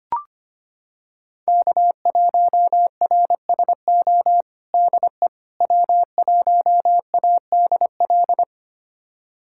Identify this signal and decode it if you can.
Morse code, heard as a keyed tone, about 25 wpm, 700 Hz. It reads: K1RSO DE W1ADL
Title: USA Callsign DE Callsign - 5 Characters - 25wpm